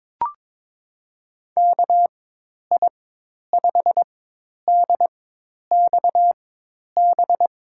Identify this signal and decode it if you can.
Morse code, heard as a keyed tone, about 22 wpm, spaced as for 10 wpm, 700 Hz. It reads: KI5DXB
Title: USA Call Signs - 6 Characters - 4x Character Spacing - 22wpm